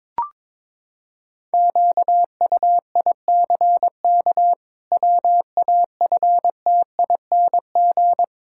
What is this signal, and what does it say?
Morse code, heard as a keyed tone, about 22 wpm, 700 Hz. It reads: QUICK WAFTING
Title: Pangram Challenge v2 - 22wpm